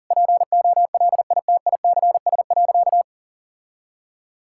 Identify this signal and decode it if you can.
Morse code, heard as a keyed tone, about 40 wpm, 700 Hz. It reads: POLITICS.
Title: Sentences from Top 1000 Words - Review Part 6 - 40wpm